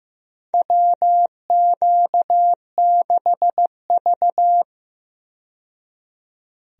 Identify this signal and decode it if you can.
Morse code, heard as a keyed tone, about 15 wpm, 700 Hz. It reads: WQ6V
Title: USA Call Signs - 4 Characters - 15wpm